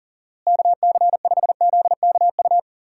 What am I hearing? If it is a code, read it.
Morse code, heard as a keyed tone, about 40 wpm, 700 Hz. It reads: KC5ZKU
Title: USA Call Signs - 6 Characters - Repeated 3x - 40wpm